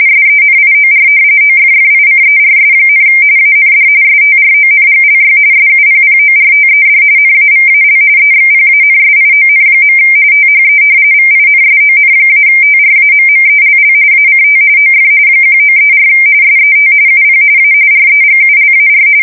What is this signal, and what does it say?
RTTY, about 45.45 baud. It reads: SENIOR OFFICERS, AND STRONGLY SUPPORTED NAVAL EXPANSION. THE VINSON-TRAMMELL ACT OF 1934 SET UP A REGULAR
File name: fsk170 / usnavy4